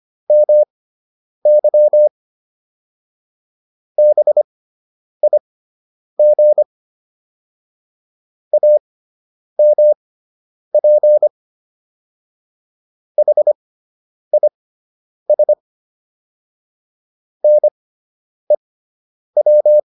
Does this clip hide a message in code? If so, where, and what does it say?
Yes: Morse code from 0.3 to 19.8 s, heard as a keyed tone, about 25 wpm, spaced as for 9 wpm, 600 Hz. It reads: MY BIG AMP HIS NEW